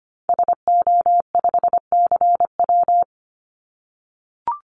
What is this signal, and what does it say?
Morse code, heard as a keyed tone, about 25 wpm, 700 Hz. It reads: SO5CW
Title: Call Signs - 25wpm